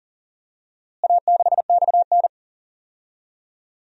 Morse code, heard as a keyed tone, about 40 wpm, 700 Hz. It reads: A6XN
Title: DXCC-100 Call Signs - 4 Characters - Repeated 3x - 40wpm